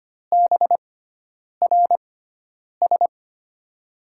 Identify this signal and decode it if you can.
Morse code, heard as a keyed tone, about 25 wpm, spaced as for 9 wpm, 700 Hz. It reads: BRS
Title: USA Call Signs - 5 Characters - 6x Character Spacing - 25wpm